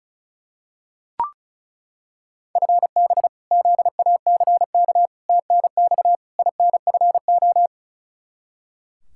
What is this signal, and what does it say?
Morse code, heard as a keyed tone, about 34 wpm, 700 Hz. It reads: FB ZACK TNX INFO
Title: Practice QSO - Exchange 3 - More Info - 35wpm